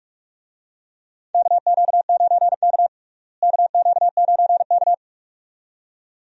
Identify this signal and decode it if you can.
Morse code, heard as a keyed tone, about 42 wpm, 700 Hz. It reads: KQ9K KQ9K
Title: USA Call Signs - 4 Characters - Repeated 2x - 45wpm